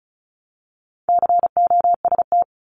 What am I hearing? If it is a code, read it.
Morse code, heard as a keyed tone, about 36 wpm, 700 Hz. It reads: COST
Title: Multi-letter Phonemes 35wpm